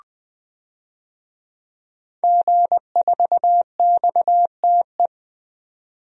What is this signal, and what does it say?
Morse code, heard as a keyed tone, about 20 wpm, 700 Hz. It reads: G4XTE
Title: DXCC-100 Call Signs - 5 Characters - 20wpm